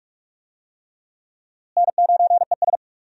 Morse code, heard as a keyed tone, about 46 wpm, 700 Hz. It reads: N9ES
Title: USA Call Signs - 4 Characters - Repeated 3x - 45wpm